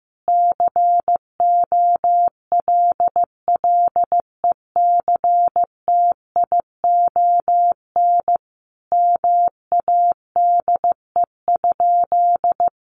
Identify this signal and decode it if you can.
Morse code, heard as a keyed tone, about 15 wpm, 700 Hz. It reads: COLLECTION MADE?
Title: Sentences from Top 1200 Words 15wpm